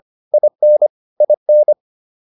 Morse code, heard as a keyed tone, about 25 wpm, 600 Hz. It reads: IN IN